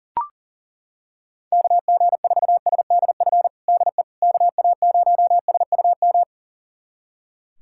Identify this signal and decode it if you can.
Morse code, heard as a keyed tone, about 37 wpm, 700 Hz. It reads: KG4SDF DE KA0SUM
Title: USA Callsign DE Callsign - 6 Characters - 40wpm